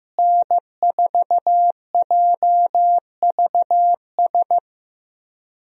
Morse code, heard as a keyed tone, about 15 wpm, 700 Hz. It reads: N4JVS
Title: USA Call Signs - 5 Characters - Repeated 3x - 15wpm